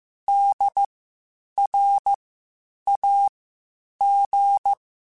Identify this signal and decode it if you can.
Morse code, heard as a keyed tone, about 15 wpm, spaced as for 8 wpm, 800 Hz. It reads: DRAG
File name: Spaced Alphas / Spaced Alphas mp3s / Ds sp @13